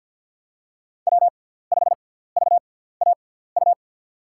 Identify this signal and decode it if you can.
Morse code, heard as a keyed tone, about 51 wpm, spaced as for 17 wpm, 700 Hz. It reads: W5VAU
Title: USA Call Signs - 5 Characters - 6x Character Spacing - 50wpm